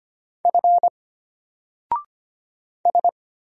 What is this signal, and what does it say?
Morse code, heard as a keyed tone, about 25 wpm, 700 Hz. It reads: F S
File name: Single-Letter-Number-25wpm